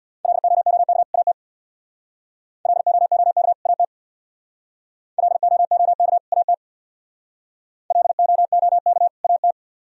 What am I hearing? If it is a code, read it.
Morse code, heard as a keyed tone, about 62 wpm, 700 Hz. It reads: LOOK AT LOOK AT LOOK AT LOOK AT
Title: Speed-Racing - Sets of 2 Words 50wpm